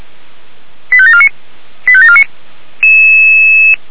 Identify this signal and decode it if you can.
Five-tone selective call (ZVEI), heard as quick runs of five tones, tones of about 70 ms. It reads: 85749 85749